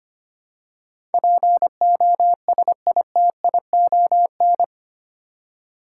Morse code, heard as a keyed tone, about 25 wpm, 700 Hz. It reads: POSITION
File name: Eight Letter Words - 25wpm